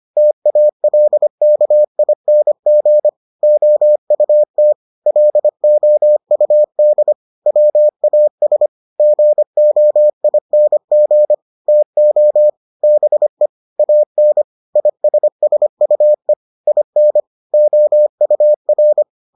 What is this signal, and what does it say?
Morse code, heard as a keyed tone, about 25 wpm, 600 Hz. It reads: TALKING OUT LOUD WAS GOING TO BE AN ISSUE IN OUR